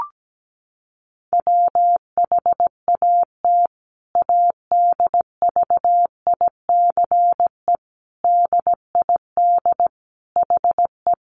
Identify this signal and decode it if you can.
Morse code, heard as a keyed tone, about 17 wpm, 700 Hz. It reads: WHAT ADVICE DID HE
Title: Sentences from Top 1100 Words 17wpm